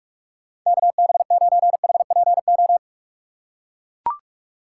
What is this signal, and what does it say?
Morse code, heard as a keyed tone, about 44 wpm, 700 Hz. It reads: K B 9 H P O
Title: USA Call Signs - 6 Characters - 6x Character Spacing - 45wpm